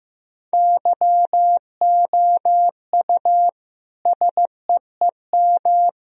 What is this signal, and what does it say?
Morse code, heard as a keyed tone, about 15 wpm, 700 Hz. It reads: YOU SEEM
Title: Sentences from Top 1000 Words - Review Part 1 - 15wpm